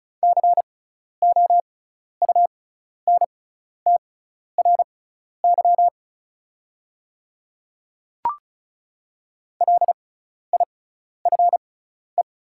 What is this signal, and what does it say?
Morse code, heard as a keyed tone, about 36 wpm, spaced as for 12 wpm, 700 Hz. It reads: COUNTRY LIFE
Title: 6x Character Spacing - Top 300 Words - 35wpm